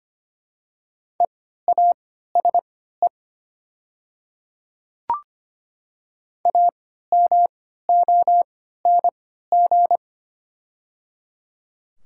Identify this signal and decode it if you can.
Morse code, heard as a keyed tone, about 25 wpm, spaced as for 14 wpm, 700 Hz. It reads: EASE AMONG
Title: 3x Character Spacing - Top 500 Words - 25wpm